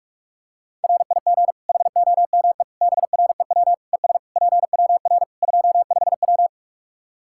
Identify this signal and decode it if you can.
Morse code, heard as a keyed tone, about 46 wpm, 700 Hz. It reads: RIG HOME BREW ES PWR 25W
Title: Practice QSO - Exchange 2 - Info - 45wpm